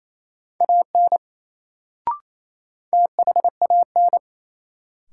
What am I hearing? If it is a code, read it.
Morse code, heard as a keyed tone, about 28 wpm, 700 Hz. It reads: AN THAN
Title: Sets of 2 Words - New Word Drill - 28wpm